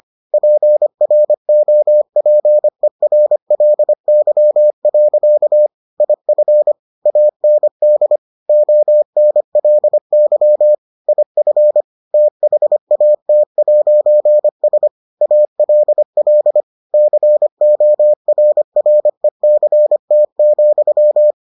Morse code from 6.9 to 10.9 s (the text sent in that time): AND ONLY